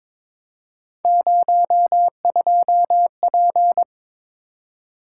Morse code, heard as a keyed tone, about 22 wpm, 700 Hz. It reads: 02P
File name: practice / 22wpm / Three Letter-Number Combinations - 22wpm